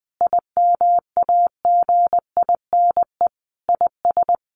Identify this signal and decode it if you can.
Morse code, heard as a keyed tone, about 20 wpm, 700 Hz. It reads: IMAGINE IS
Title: Sentences from Top 1400 Words 20wpm